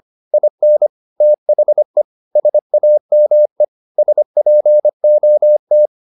Morse code, heard as a keyed tone, about 25 wpm, 600 Hz. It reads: IN THE SAME SPOT